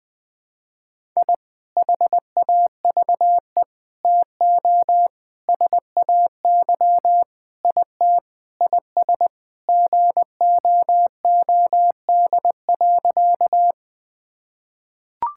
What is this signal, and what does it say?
Morse code, heard as a keyed tone, about 20 wpm, 700 Hz. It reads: I HAVE TO SAY IT IS GOOD.
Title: Sentences with 4-Word Phrases - 20wpm